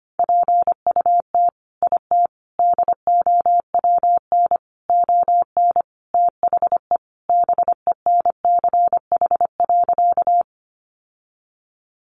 Morse code, heard as a keyed tone, about 25 wpm, 700 Hz. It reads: PUT IT DOWN ON THE BENCH.
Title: Sentences from Top 1400 Words 25wpm